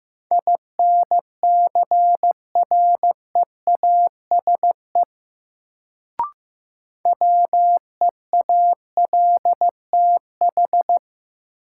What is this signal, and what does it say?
Morse code, heard as a keyed tone, about 15 wpm, 700 Hz. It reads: INCREASE WEALTH
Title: Sentences from Top 1200 Words - New Word Drill Alt - 15wpm